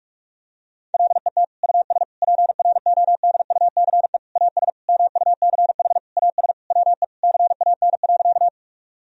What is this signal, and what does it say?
Morse code, heard as a keyed tone, about 47 wpm, 700 Hz. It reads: LET US PRODUCE AS MUCH AS WE CAN.
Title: Sentences from Top 1000 Words - Review Part 3 - 45wpm